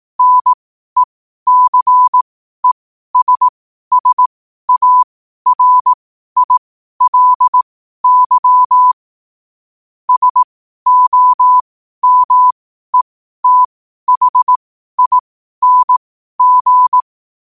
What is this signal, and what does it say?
Morse code, heard as a keyed tone, about 18 wpm, spaced as for 12 wpm, 1000 Hz. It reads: NECESSARILY SOMETHING
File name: qotdfiles / 12wpm